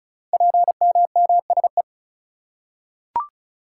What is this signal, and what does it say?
Morse code, heard as a keyed tone, about 36 wpm, 700 Hz. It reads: PMMSE
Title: Code Groups - 5 Letters - 35wpm